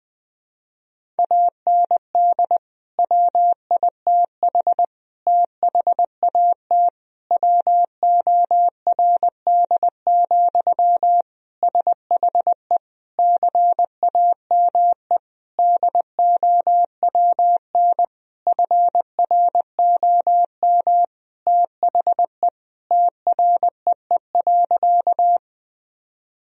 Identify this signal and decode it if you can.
Morse code, heard as a keyed tone, about 21 wpm, 700 Hz. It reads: AND WITH THAT WORD, SHE CAME DOWN FROM THE TREE.
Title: Sentences from Top 1000 Words - Review Part 1 - 20wpm